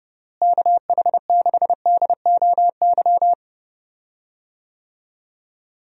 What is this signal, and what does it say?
Morse code, heard as a keyed tone, about 31 wpm, 700 Hz. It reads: KH6DOY
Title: DXCC-100 Call Signs - 6 Characters - Repeated 2x - 30wpm